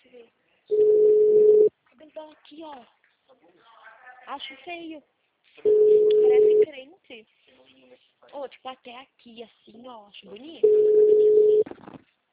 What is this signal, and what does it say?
Ringback tone (Europe, 425 Hz), with speech.